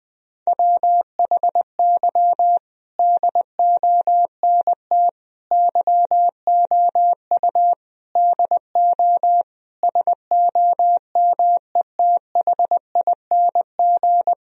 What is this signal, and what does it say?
Morse code, heard as a keyed tone, about 20 wpm, 700 Hz. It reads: WHY DONT YOU DO SOMETHING
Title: Sentences from Top 1000 Words - Review Part 4 - 20wpm